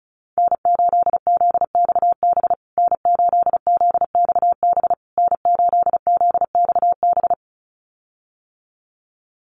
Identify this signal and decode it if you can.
Morse code, heard as a keyed tone, about 34 wpm, 700 Hz. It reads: N8ZXB N8ZXB N8ZXB
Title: USA Call Signs Repeated 3x 35wpm